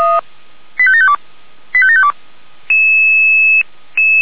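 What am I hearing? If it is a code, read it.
Five-tone selective call (ZVEI), heard as quick runs of five tones, tones of about 70 ms. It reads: 85752 85752
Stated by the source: ZVEI five-tone call 85827 received by SDR.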